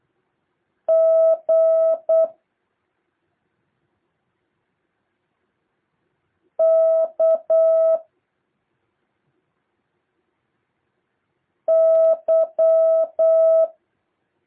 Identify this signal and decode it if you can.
Morse code, heard as a keyed tone, about 8 wpm, 650 Hz. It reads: G K Y
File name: jh8cbh / 2023cw